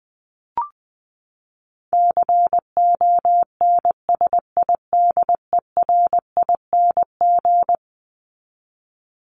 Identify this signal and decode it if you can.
Morse code, heard as a keyed tone, about 20 wpm, 700 Hz. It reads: CONSIDERING
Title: ING Suffix - Top 100 Words with Base Words 20wpm